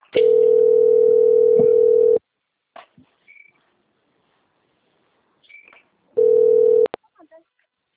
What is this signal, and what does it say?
Ringback tone (North America), 440 + 480 Hz.